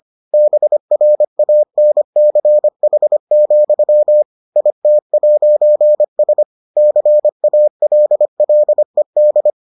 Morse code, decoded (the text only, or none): BRANCH, IT'S CALLED